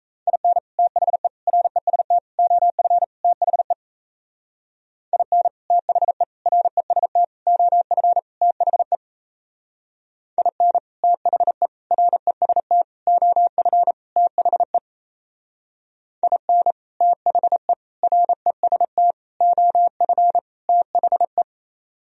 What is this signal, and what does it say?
Morse code, heard as a keyed tone, about 37 wpm, 700 Hz. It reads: IN THE REST OF THE IN THE REST OF THE IN THE REST OF THE IN THE REST OF THE